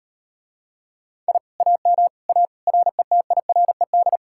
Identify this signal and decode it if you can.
Morse code, heard as a keyed tone, about 38 wpm, 700 Hz. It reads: I AM A RETIRED